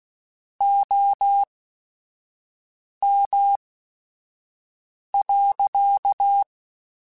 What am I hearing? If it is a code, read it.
Morse code, heard as a keyed tone, about 16 wpm, spaced as for 5 wpm, 800 Hz. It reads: OM.